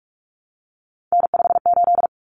Morse code, heard as a keyed tone, about 44 wpm, 700 Hz. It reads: N58
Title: Common Cut Numbers - 3-Digit 45wpm